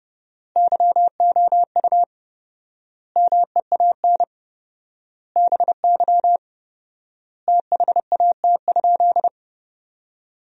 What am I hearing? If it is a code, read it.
Morse code, heard as a keyed tone, about 32 wpm, 700 Hz. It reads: YOU MEAN BY THAT?
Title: 4x Word-Spacing - Sentences from Top 200 Words - 30wpm